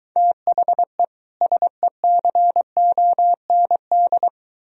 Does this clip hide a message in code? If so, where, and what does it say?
Yes: Morse code from 0.2 to 4.3 s, heard as a keyed tone, about 23 wpm, 700 Hz. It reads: THE SECOND